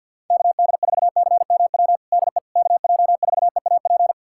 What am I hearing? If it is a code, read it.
Morse code, heard as a keyed tone, about 52 wpm, 700 Hz. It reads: KD4CMW DE KJ4EAP
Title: USA Callsign DE Callsign 50wpm